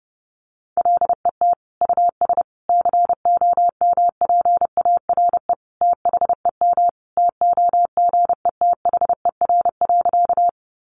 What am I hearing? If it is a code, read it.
Morse code, heard as a keyed tone, about 30 wpm, 700 Hz. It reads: LET US COMPARE THEM TOGETHER.
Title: Sentences from Top 1300 Words 30wpm